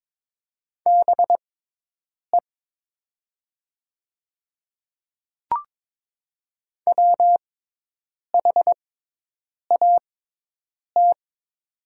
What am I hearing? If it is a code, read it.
Morse code, heard as a keyed tone, about 22 wpm, spaced as for 8 wpm, 700 Hz. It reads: BE WHAT
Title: 6x Character Spacing - Top 200 Words - 22wpm